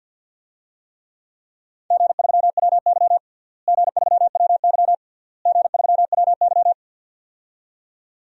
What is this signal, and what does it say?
Morse code, heard as a keyed tone, about 46 wpm, 700 Hz. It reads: G3WY G3WY G3WY
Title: DXCC-100 Call Signs - 4 Characters - Repeated 3x - 50wpm